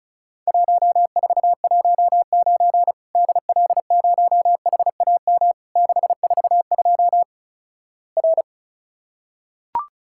Morse code, heard as a keyed tone, about 36 wpm, 700 Hz. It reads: 1419 DL0HAM 642 R
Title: Worked All Europe DX Contest - QTC Exchange - 35wpm